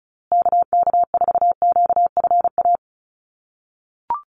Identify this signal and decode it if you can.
Morse code, heard as a keyed tone, about 34 wpm, 700 Hz. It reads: KK4QFA
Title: USA Call Signs Repeated 3x 35wpm